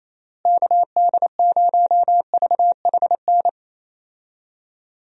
Morse code, heard as a keyed tone, about 27 wpm, 700 Hz. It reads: KD0VHN
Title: USA Call Signs - 6 Characters - Repeated 2x - 28wpm